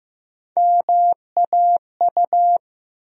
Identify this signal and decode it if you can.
Morse code, heard as a keyed tone, about 15 wpm, 700 Hz. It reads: MAU
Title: USA Call Sign Postfixes - 3 Letters 15wpm